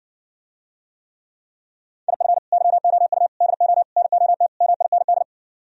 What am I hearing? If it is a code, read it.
Morse code, heard as a keyed tone, about 59 wpm, 700 Hz. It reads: IF YOU DO NOT MIND